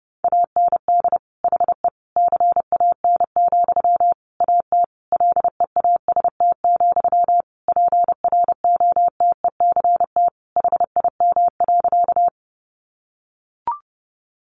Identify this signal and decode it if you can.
Morse code, heard as a keyed tone, about 30 wpm, 700 Hz. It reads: AND HE CAN, AT LEAST, PROTECT HIM.
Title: Sentences from Top 900 Words 30wpm